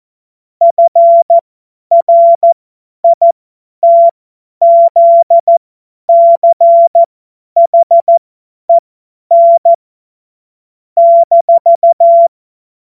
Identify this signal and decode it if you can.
Morse code, heard as a keyed tone, about 14 wpm, spaced as for 10 wpm, 680 Hz. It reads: FRITZCHEN -
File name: Morsekurs / Witze / Witz 10. August 2021, 50 F70